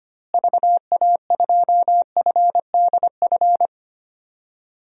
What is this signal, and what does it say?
Morse code, heard as a keyed tone, about 25 wpm, 700 Hz. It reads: VA2FDF
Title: DXCC-100 Call Signs - 6 Characters - Repeated 2x - 25wpm